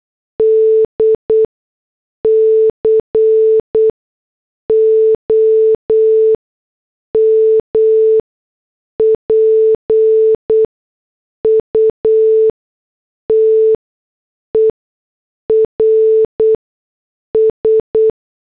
Morse code, heard as a keyed tone, about 8 wpm, spaced as for 6 wpm, 440 Hz. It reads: D COMPUTERS
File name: morse / headlines